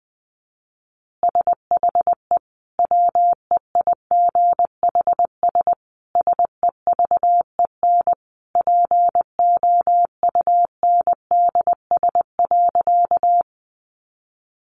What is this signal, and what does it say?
Morse code, heard as a keyed tone, about 20 wpm, 700 Hz. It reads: SHE WEIGHS SEVEN POUNDS.
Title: Easy Phrases 20wpm